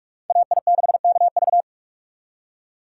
Morse code, heard as a keyed tone, about 58 wpm, 700 Hz. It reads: AI6KV